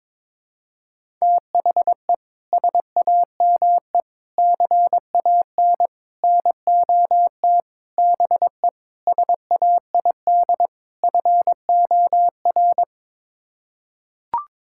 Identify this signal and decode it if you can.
Morse code, heard as a keyed tone, about 22 wpm, 700 Hz. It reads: THE SAME CAN NOT BE SAID FOR
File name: practice / 22wpm / Sets of 7 Words - 22wpm